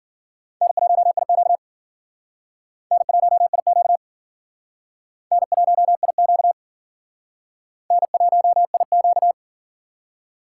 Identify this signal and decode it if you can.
Morse code, heard as a keyed tone, about 52 wpm, 700 Hz. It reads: N1IQ N1IQ N1IQ N1IQ